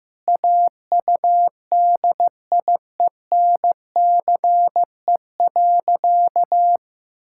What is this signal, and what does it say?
Morse code, heard as a keyed tone, about 16 wpm, 700 Hz. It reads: AUDIENCE.
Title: Sentences from Top 1000 Words - Review Part 5 - 15wpm